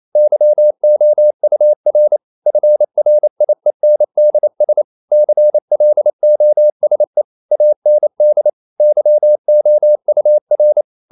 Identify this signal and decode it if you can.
Morse code, heard as a keyed tone, about 28 wpm, 600 Hz. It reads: YOUR FRIENDS CLOSE AND YOUR
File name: CW / Practice / EnglishSayings28